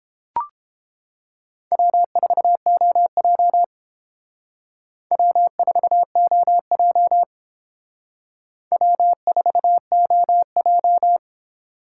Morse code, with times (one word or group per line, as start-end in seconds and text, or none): 1.7-3.6 W4OJ
5.1-7.2 W4OJ
8.7-11.2 W4OJ